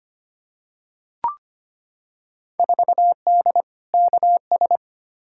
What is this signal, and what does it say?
Morse code, heard as a keyed tone, about 25 wpm, 700 Hz. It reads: 4D KS
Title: Field Day Exchanges Encore - 25wpm